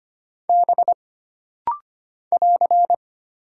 Morse code, heard as a keyed tone, about 25 wpm, 700 Hz. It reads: B <AR>
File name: Single Character (Letter:Number:Punctuation:Prosign) - Rapid-Fire - 25wpm